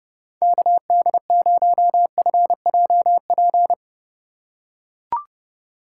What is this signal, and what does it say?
Morse code, heard as a keyed tone, about 30 wpm, 700 Hz. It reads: KD0FJP
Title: USA Call Signs - 6 Characters - 2x Character Spacing - 30wpm